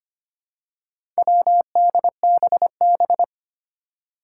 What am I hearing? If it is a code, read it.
Morse code, heard as a keyed tone, about 25 wpm, 700 Hz. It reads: WDBB
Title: Four Challenging Letter-Number Combinations 25wpm